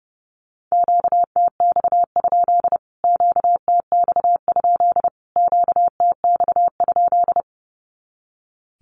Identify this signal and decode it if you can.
Morse code, heard as a keyed tone, about 30 wpm, 700 Hz. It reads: QTX? QTX? QTX?
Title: Q-Codes Repeated 3x 30wpm